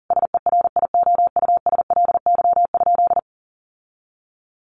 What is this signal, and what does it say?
Morse code, heard as a keyed tone, about 40 wpm, 700 Hz. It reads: SERIOUSLY?
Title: Sentences from Top 1200 Words 40wpm